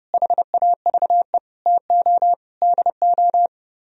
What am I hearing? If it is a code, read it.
Morse code, heard as a keyed tone, about 33 wpm, 700 Hz. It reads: HAVE TO DO